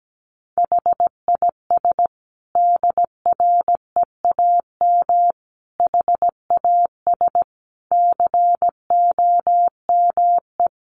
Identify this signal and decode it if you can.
Morse code, heard as a keyed tone, about 17 wpm, 700 Hz. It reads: HIS DREAM HAS COME
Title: Sentences from Top 1500 Words 17wpm